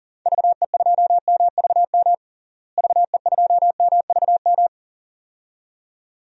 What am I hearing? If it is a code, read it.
Morse code, heard as a keyed tone, about 40 wpm, 700 Hz. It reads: VE2MVM VE2MVM
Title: DXCC-100 Call Signs - 6 Characters - Repeated 2x - 40wpm